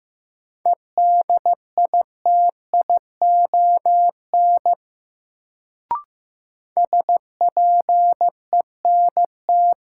Morse code, heard as a keyed tone, about 15 wpm, 700 Hz. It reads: EDITION SPENT
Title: Sets of 5 Words - New Word Drill - 15wpm